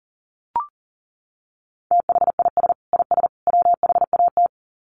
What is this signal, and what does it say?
Morse code, heard as a keyed tone, about 40 wpm, 700 Hz. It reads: THIS IS WHAT
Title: Sets of 3 Words 40wpm